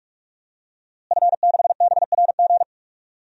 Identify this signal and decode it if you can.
Morse code, heard as a keyed tone, about 45 wpm, 700 Hz. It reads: F6BRG
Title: DXCC-100 Call Signs - 5 Characters - Repeated 2x - 45wpm